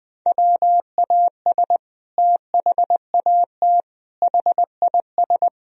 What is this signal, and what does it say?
Morse code, heard as a keyed tone, about 20 wpm, 700 Hz. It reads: WAS THAT HIS
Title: Sentences from Top 1000 Words - Review Part 1 - 20wpm